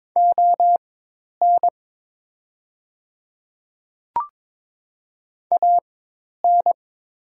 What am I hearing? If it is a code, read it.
Morse code, heard as a keyed tone, about 22 wpm, spaced as for 10 wpm, 690 Hz. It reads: ON AN
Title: 4x Character Spacing - Top 100 Words - 22wpm